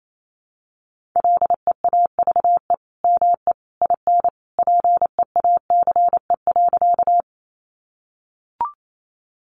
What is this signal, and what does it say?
Morse code, heard as a keyed tone, about 28 wpm, 700 Hz. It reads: LEAVE ME IN PEACE.
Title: Sentences from Top 1500 Words 28wpm